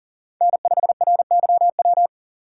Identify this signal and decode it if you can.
Morse code, heard as a keyed tone, about 40 wpm, 700 Hz. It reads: N5RYW
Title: USA Call Signs - 5 Characters - 2x Character Spacing - 40wpm